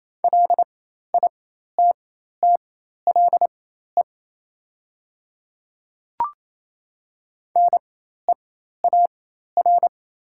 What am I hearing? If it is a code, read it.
Morse code, heard as a keyed tone, about 28 wpm, spaced as for 13 wpm, 700 Hz. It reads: LITTLE NEAR
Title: 4x Character Spacing - Top 200 Words - 28wpm